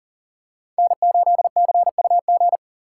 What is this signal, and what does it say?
Morse code, heard as a keyed tone, about 40 wpm, 700 Hz. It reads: N8CUG
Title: USA Call Signs - 5 Characters - 3x Character Spacing - 40wpm